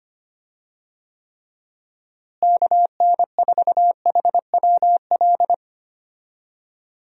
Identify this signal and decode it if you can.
Morse code, heard as a keyed tone, about 25 wpm, 700 Hz. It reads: KN4HWL
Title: USA Call Signs - 6 Characters - Repeated 3x - 25wpm